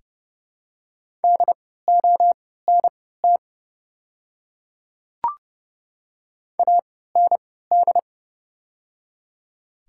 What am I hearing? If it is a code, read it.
Morse code, heard as a keyed tone, about 30 wpm, spaced as for 17 wpm, 690 Hz. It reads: DONT AND
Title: 3x Character Spacing - Top 300 Words - 30wpm